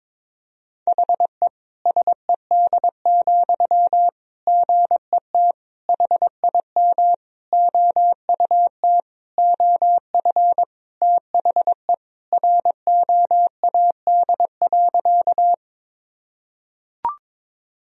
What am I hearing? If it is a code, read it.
Morse code, heard as a keyed tone, about 22 wpm, 700 Hz. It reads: HE SED, GET HIM OUT OF THE ROAD.
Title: Sentences from Top 400 Words and Abbreviations 22wpm